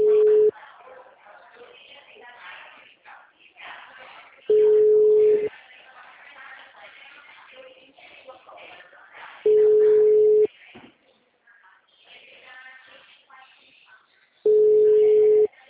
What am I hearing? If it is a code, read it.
Ringback tone (Europe, 425 Hz), with speech.